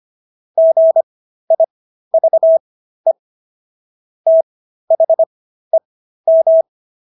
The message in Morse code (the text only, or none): GIVE THEM